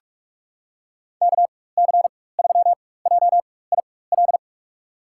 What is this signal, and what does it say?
Morse code, heard as a keyed tone, about 42 wpm, spaced as for 21 wpm, 700 Hz. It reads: KC3JIL
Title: USA Call Signs - 4x Character Spacing - 45wpm